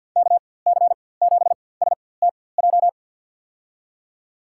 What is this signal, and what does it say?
Morse code, heard as a keyed tone, about 47 wpm, spaced as for 23 wpm, 700 Hz. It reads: KC7STJ
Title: USA Call Signs - 6 Characters - 4x Character Spacing - 50wpm